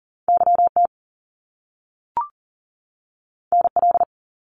Morse code, heard as a keyed tone, about 40 wpm, 700 Hz. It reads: YT NL
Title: Canadian Provinces and Territories 40wpm